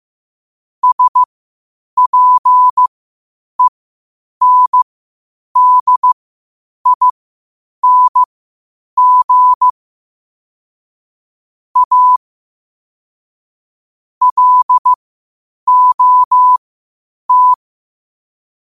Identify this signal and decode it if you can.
Morse code, heard as a keyed tone, about 15 wpm, spaced as for 8 wpm, 1000 Hz. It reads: SPENDING A LOT
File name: qotdfiles / 8wpm